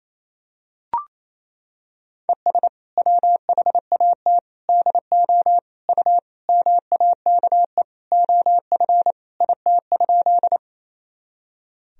Morse code, heard as a keyed tone, about 28 wpm, 700 Hz. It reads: ES WHAT DO U MAKE OF IT?
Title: Sentences from Top 100 Words and Abbreviations 28wpm